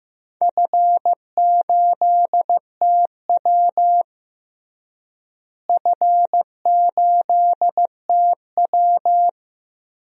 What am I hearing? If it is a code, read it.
Morse code, heard as a keyed tone, about 15 wpm, 700 Hz. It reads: F8TW F8TW
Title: DXCC-100 Call Signs - 4 Characters - Repeated 2x - 15wpm